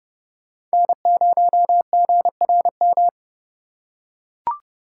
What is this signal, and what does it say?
Morse code, heard as a keyed tone, about 30 wpm, 700 Hz. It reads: N0GRM